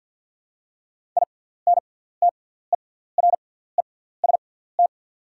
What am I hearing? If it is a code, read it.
Morse code, heard as a keyed tone, about 48 wpm, spaced as for 17 wpm, 700 Hz. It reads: INTEREST
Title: 6x Character Spacing - Top 500 Words - 50wpm